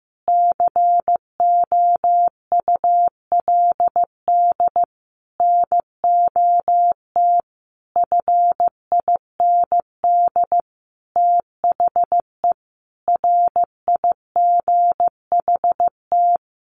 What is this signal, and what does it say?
Morse code, heard as a keyed tone, about 15 wpm, 700 Hz. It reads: COULD NOT FIND THE RIGHT
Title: Sentences from Top 600 Words 15wpm